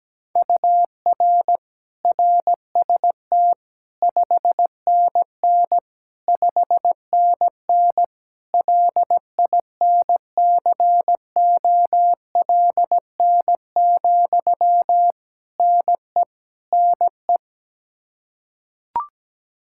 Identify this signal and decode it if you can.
Morse code, heard as a keyed tone, about 17 wpm, 700 Hz. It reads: UR RST 5NN 5NN LINCOLN, NE NE
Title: Practice QSO - Exchange 1 - Beginning - 17wpm